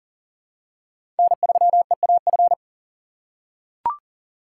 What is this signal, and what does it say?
Morse code, heard as a keyed tone, about 40 wpm, 700 Hz. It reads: N3EAF
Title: USA Call Signs - 6x Character Spacing - 40wpm